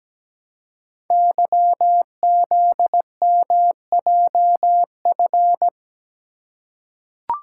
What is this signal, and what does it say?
Morse code, heard as a keyed tone, about 17 wpm, 700 Hz. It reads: YZMJF
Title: Code Groups - 5 Letters - 17wpm